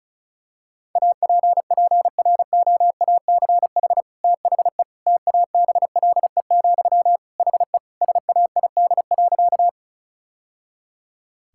Morse code, heard as a keyed tone, about 36 wpm, 700 Hz. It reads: APPROACH THE TABLE, HE SAID.
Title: Sentences from Top 1000 Words - Review Part 6 - 35wpm